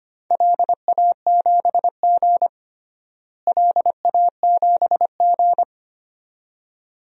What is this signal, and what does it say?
Morse code, heard as a keyed tone, about 25 wpm, 700 Hz. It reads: LA7G LA7G
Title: DXCC-100 Call Signs - 4 Characters - Repeated 2x - 25wpm